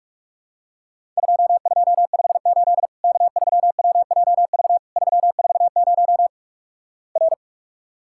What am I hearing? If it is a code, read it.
Morse code, heard as a keyed tone, about 46 wpm, 700 Hz. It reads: 2258 K3WJV 340 R
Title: Worked All Europe DX Contest - QTC Exchange - 45wpm